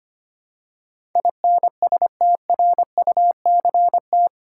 Morse code, heard as a keyed tone, about 25 wpm, 700 Hz. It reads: INSTRUCT
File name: practice / 25wpm / 8 Letter Words - First 3 Letters Given - 25wpm